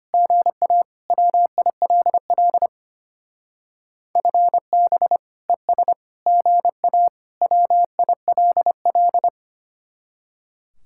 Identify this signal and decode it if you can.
Morse code, heard as a keyed tone, about 32 wpm, 700 Hz. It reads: GA WILL FB ES GA WILL